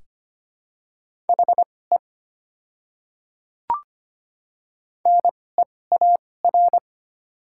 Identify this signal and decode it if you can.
Morse code, heard as a keyed tone, about 25 wpm, spaced as for 18 wpm, 700 Hz. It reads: HE NEAR
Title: 2x Character Spacing - Top 300 Words - 25wpm